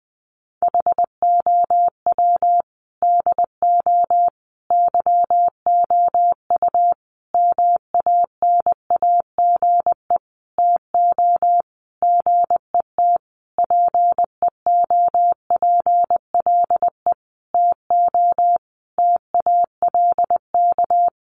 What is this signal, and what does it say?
Morse code, heard as a keyed tone, about 20 wpm, 700 Hz. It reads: HOW DO YOU MANAGE TO GET PEOPLE TO TALK
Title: Sentences from Top 1400 Words 20wpm